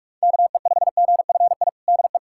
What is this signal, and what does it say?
Morse code, heard as a keyed tone, about 46 wpm, 700 Hz. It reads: KE5GFI DE
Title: USA Callsign DE Callsign 45wpm